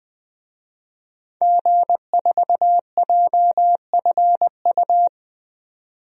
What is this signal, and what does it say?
Morse code, heard as a keyed tone, about 20 wpm, 700 Hz. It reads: G4JFU
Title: DXCC-100 Call Signs - 5 Characters - 20wpm